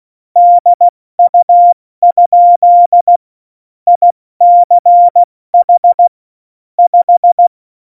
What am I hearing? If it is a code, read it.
Morse code, heard as a keyed tone, about 16 wpm, 700 Hz. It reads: DU? ICH 5
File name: Morsekurs / Witze / Witz 18 August 2020, 70 F80